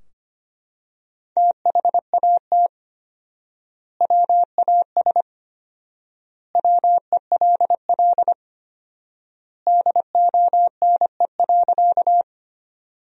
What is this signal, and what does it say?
Morse code, heard as a keyed tone, about 25 wpm, 700 Hz. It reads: THAT WAS WELL DONE.
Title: 4x Word-Spacing - Sentences from Top 500 Words - 25wpm